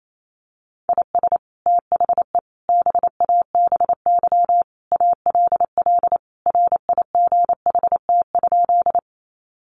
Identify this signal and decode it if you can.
Morse code, heard as a keyed tone, about 28 wpm, 700 Hz. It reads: IS THE BABY ALL RIGHT?
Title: Sentences from Top 1300 Words 28wpm